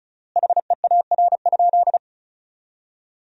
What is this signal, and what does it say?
Morse code, heard as a keyed tone, about 36 wpm, 700 Hz. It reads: HEAR?
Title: 3x Word-Spacing - Sentences from Top 300 Words - 35wpm